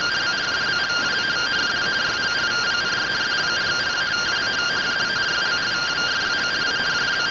Two-tone teleprinter signal, about 170 Hz shift.